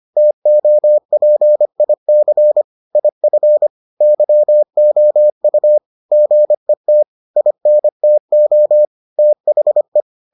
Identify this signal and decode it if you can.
Morse code, heard as a keyed tone, about 25 wpm, 600 Hz. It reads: TOPIC IF YOU GET INTO THE